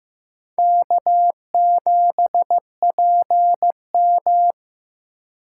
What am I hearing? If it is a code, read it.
Morse code, heard as a keyed tone, about 18 wpm, 700 Hz. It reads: K7PM
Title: Speed Racing - USA 1x2 Call Signs 15wpm